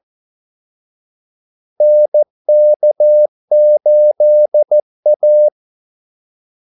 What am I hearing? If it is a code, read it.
Morse code, heard as a keyed tone, about 14 wpm, 600 Hz. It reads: NK8A
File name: Callsigns-14wpm